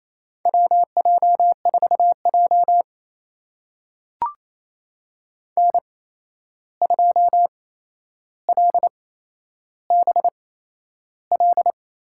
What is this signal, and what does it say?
Morse code, heard as a keyed tone, about 28 wpm, spaced as for 11 wpm, 700 Hz. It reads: WJ4J N2LBL
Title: USA Call Signs - 8x Character Spacing - 28wpm